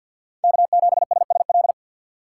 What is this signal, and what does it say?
Morse code, heard as a keyed tone, about 48 wpm, 690 Hz. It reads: K7SSL
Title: USA Call Signs - 5 Characters - 6x Character Spacing - 50wpm